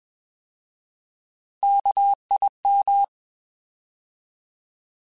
Morse code, heard as a keyed tone, about 21 wpm, 800 Hz. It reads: KIM
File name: Alphabetical / Alphabetical mp3s / Ks @20